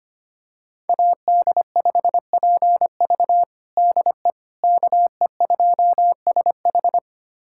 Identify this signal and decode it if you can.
Morse code, heard as a keyed tone, about 25 wpm, 700 Hz. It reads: AD5PV DE KE2SH
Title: USA Callsign DE Callsign - 5 Characters - 25wpm